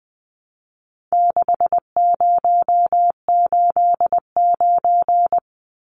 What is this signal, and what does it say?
Morse code, heard as a keyed tone, about 20 wpm, 700 Hz. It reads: 6089
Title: Numbers - 4-Digit 20wpm